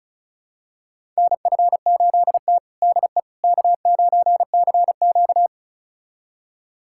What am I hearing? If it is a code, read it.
Morse code, heard as a keyed tone, about 34 wpm, 700 Hz. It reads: NF8T DE K9CQ
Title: USA Callsign DE Callsign - 4 Characters - 35wpm